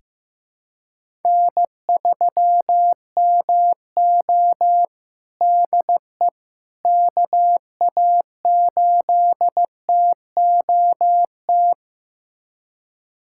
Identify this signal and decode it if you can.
Morse code, heard as a keyed tone, about 14 wpm, 700 Hz. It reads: N3MO DE KA8TOT
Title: USA Callsign DE Callsign 15wpm